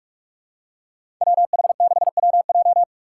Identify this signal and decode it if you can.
Morse code, heard as a keyed tone, about 46 wpm, 700 Hz. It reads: WH6WJ
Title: USA Call Signs - 3x Character Spacing - 45wpm